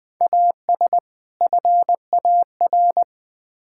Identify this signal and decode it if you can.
Morse code, heard as a keyed tone, about 20 wpm, 700 Hz. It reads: AS FAR
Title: Build A Phrase - 5-Word Phrases - 20wpm